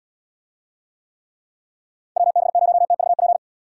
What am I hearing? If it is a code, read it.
Morse code, heard as a keyed tone, about 79 wpm, 700 Hz. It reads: WB0TVP